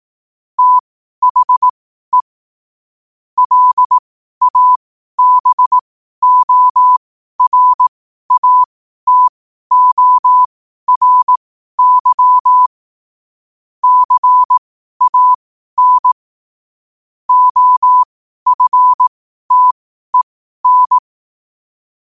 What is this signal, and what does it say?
Morse code, heard as a keyed tone, about 18 wpm, spaced as for 12 wpm, 1000 Hz. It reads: THE LABORATORY CAN OFTEN
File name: qotdfiles / 12wpm